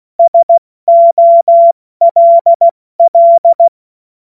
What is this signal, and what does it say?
Morse code, heard as a keyed tone, about 16 wpm, 680 Hz. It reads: SOLL
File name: Morsekurs / Witze / Witz 02 April 2024 70 F80